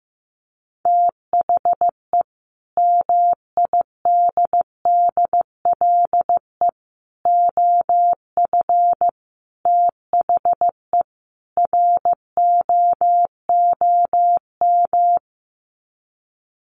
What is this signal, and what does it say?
Morse code, heard as a keyed tone, about 15 wpm, 700 Hz. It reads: THE MIDDLE OF THE ROOM
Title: Sets of 5 Words Encore2 15wpm